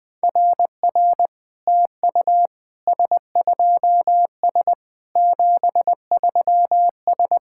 Morse code, heard as a keyed tone, about 20 wpm, 700 Hz. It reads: RR TU S2S 73S
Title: SOTA QSO Practice 20wpm